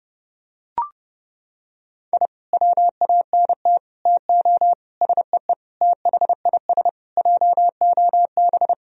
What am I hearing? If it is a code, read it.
Morse code, heard as a keyed tone, about 30 wpm, 700 Hz. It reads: I WANT TO SEE THIS JOB
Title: Sentences from Top 1000 Words - Review Part 3 - 30wpm